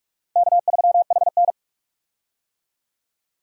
Morse code, heard as a keyed tone, about 46 wpm, 700 Hz. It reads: K3HN
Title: USA Call Signs - 4 Characters - Repeated 3x - 45wpm